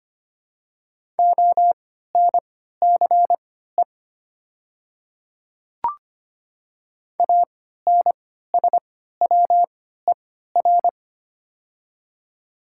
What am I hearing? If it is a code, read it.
Morse code, heard as a keyed tone, about 25 wpm, spaced as for 14 wpm, 700 Hz. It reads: ONCE ANSWER
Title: 3x Character Spacing - Top 300 Words - 25wpm